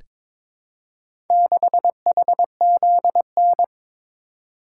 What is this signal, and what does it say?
Morse code, heard as a keyed tone, about 23 wpm, 700 Hz. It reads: 6HZN
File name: practice / 22wpm / Four Letter-Number Combinations - 22wpm